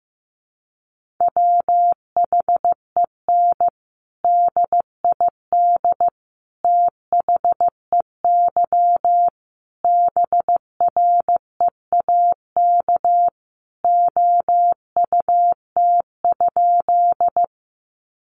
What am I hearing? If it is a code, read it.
Morse code, heard as a keyed tone, about 15 wpm, 700 Hz. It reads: WHEN DID THEY BREAK OUT?
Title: Sentences from Top 900 Words 15wpm